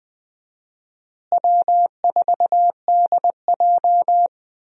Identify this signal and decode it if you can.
Morse code, heard as a keyed tone, about 20 wpm, 690 Hz. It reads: W4DJ
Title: USA Call Signs - 4 Characters - Repeated 3x - 20wpm